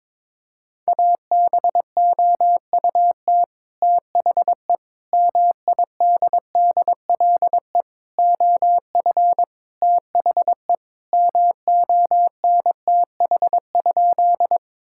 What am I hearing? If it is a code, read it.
Morse code, heard as a keyed tone, about 22 wpm, 700 Hz. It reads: ABOUT THE MIDDLE OF THE MONTH?
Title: Sentences from Top 1000 Words - Review Part 6 - 22wpm